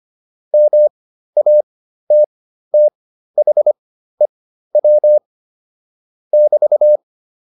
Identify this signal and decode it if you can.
Morse code, heard as a keyed tone, about 25 wpm, spaced as for 13 wpm, 600 Hz. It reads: MATTHEW <BT>